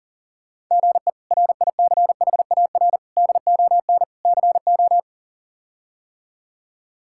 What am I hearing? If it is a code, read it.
Morse code, heard as a keyed tone, about 40 wpm, 690 Hz. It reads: GE RICHAR DON CO
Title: K1USN SST QSO Practice - 40wpm